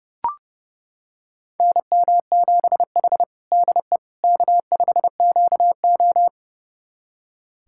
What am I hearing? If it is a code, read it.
Morse code, heard as a keyed tone, about 30 wpm, 700 Hz. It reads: NM7H DE K5QO
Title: USA Callsign DE Callsign - 4 Characters - 30wpm